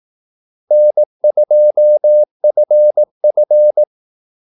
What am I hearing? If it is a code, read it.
Morse code, heard as a keyed tone, about 18 wpm, 600 Hz. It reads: N2FF
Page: Kaldesignaler 18 wpm